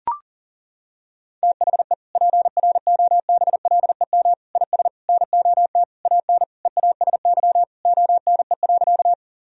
Morse code, heard as a keyed tone, about 40 wpm, 700 Hz. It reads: THE PROBLEM IS NOT AN EASY ONE.
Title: Sentences from Top 1000 Words - Review Part 2 - 40wpm